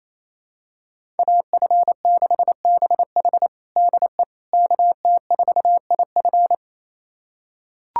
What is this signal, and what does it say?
Morse code, heard as a keyed tone, about 28 wpm, 700 Hz. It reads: AF6BH DE KT4IF
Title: USA Callsign DE Callsign - 5 Characters - 28wpm